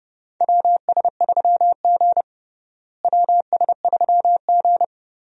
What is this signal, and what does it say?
Morse code, heard as a keyed tone, about 30 wpm, 700 Hz. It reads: WS3G WS3G
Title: USA Call Signs - 4 Characters - Repeated 3x - 30wpm